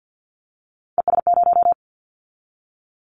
Morse code, heard as a keyed tone, about 50 wpm, 700 Hz. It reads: ES0
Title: Callsign Prefixes for Top 50 DXCC - 50wpm